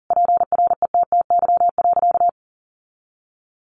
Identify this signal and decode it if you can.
Morse code, heard as a keyed tone, about 40 wpm, 700 Hz. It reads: PRETTY.
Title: Sentences from Top 1000 Words 40wpm